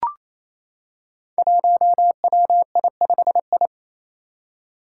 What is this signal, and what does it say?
Morse code, heard as a keyed tone, about 28 wpm, 700 Hz. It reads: 1WI5I
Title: Five Challenging Letter-Number Combinations 28wpm